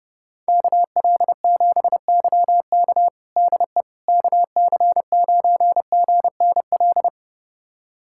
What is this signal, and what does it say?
Morse code, heard as a keyed tone, about 31 wpm, 700 Hz. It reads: KL7YK DE KC9GNL
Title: USA Callsign DE Callsign 30wpm